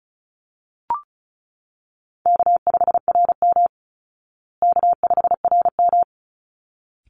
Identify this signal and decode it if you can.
Morse code, heard as a keyed tone, about 34 wpm, 700 Hz. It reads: K5RM K5RM
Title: Easy Call Signs Repeated 2x - 35wpm